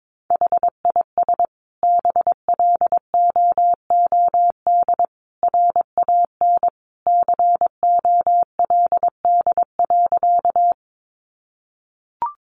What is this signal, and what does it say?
Morse code, heard as a keyed tone, about 22 wpm, 700 Hz. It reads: HIS BLOOD RAN COLD.
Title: Sentences from Top 1100 Words 22wpm